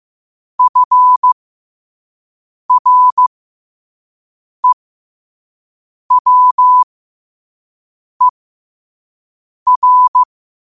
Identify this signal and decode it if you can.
Morse code, heard as a keyed tone, about 15 wpm, spaced as for 5 wpm, 1000 Hz. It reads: FREWER
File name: qotdfiles / 5wpm